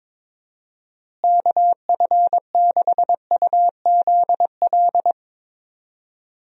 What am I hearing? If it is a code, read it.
Morse code, heard as a keyed tone, about 21 wpm, 700 Hz. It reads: KF6UZL
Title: USA Call Signs - 6 Characters - Repeated 2x - 22wpm